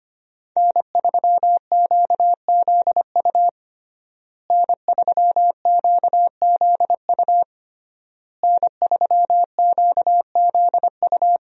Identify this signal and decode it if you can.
Morse code, heard as a keyed tone, about 25 wpm, 700 Hz. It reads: N3QZU N3QZU N3QZU
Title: USA Call Signs - 5 Characters - Repeated 3x - 25wpm